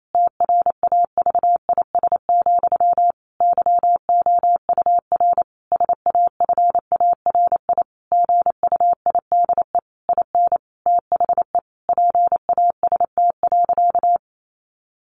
Morse code, heard as a keyed tone, about 28 wpm, 700 Hz. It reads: TRAVIS, YOUR SAFARI GUIDE IN THE PAST.
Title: A Sound Of Thunder 28wpm